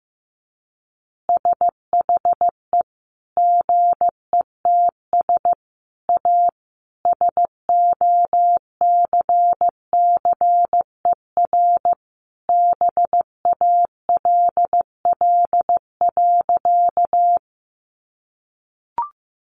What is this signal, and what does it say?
Morse code, heard as a keyed tone, about 15 wpm, 700 Hz. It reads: SHE GETS A SOCCER BALL.
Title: Easy Phrases 15wpm